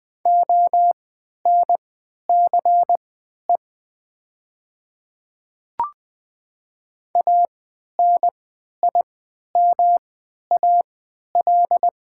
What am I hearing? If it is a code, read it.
Morse code, heard as a keyed tone, about 20 wpm, spaced as for 11 wpm, 700 Hz. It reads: ONCE ANIMAL
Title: 3x Character Spacing - Top 300 Words - 20wpm